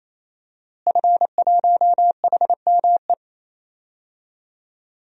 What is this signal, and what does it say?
Morse code, heard as a keyed tone, about 28 wpm, 700 Hz. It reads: F1HME
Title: DXCC-100 Call Signs - 5 Characters - Repeated 3x - 28wpm